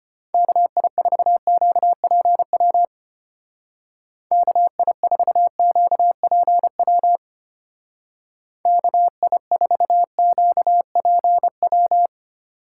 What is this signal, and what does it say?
Morse code, heard as a keyed tone, about 30 wpm, 700 Hz. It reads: KI4QPW KI4QPW KI4QPW
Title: Speed Racing - USA Call Signs 25wpm